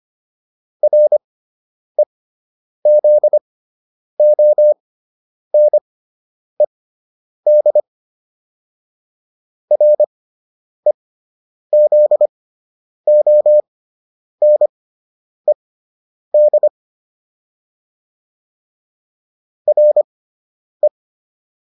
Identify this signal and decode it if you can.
Morse code, heard as a keyed tone, about 25 wpm, spaced as for 9 wpm, 600 Hz. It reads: REZONED REZONED RE